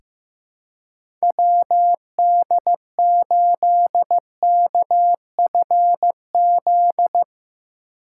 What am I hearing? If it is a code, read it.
Morse code, heard as a keyed tone, about 15 wpm, 700 Hz. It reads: WD8KFZ
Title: USA Call Signs - 6 Characters - 4x Character Spacing - 15wpm